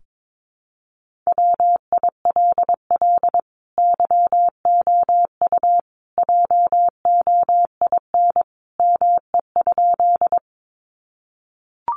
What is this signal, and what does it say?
Morse code, heard as a keyed tone, about 22 wpm, 700 Hz. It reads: WILL YOU JOIN ME?
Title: Sentences from Top 1400 Words 22wpm